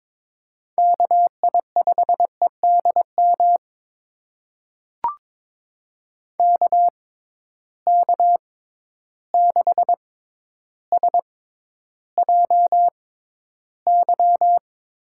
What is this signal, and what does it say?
Morse code, heard as a keyed tone, about 21 wpm, spaced as for 11 wpm, 700 Hz. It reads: KI5EDM KK6SJY
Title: USA Call Signs - 6 Characters - 6x Character Spacing - 22wpm